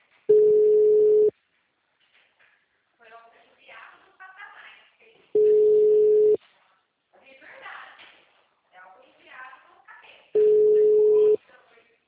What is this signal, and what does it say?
Ringback tone (Europe, 425 Hz), with speech.